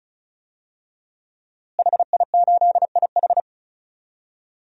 Morse code, heard as a keyed tone, about 34 wpm, 700 Hz. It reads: HI8IH
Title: DXCC-100 Call Signs - 5 Characters - Repeated 2x - 35wpm